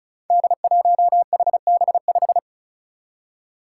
Morse code, heard as a keyed tone, about 36 wpm, 700 Hz. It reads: D1HB5
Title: Five Challenging Letter-Number Combinations 35wpm